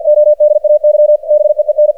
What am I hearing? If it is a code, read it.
Morse code, heard as a keyed tone, about 47 wpm, 600 Hz. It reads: ODAY BEEN
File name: archived / CRSnet